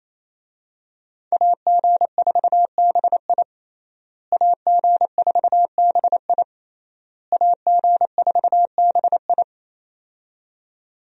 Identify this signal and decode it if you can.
Morse code, heard as a keyed tone, about 27 wpm, 700 Hz. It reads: AG4BI AG4BI AG4BI
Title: USA Call Signs - 5 Characters - Repeated 3x - 28wpm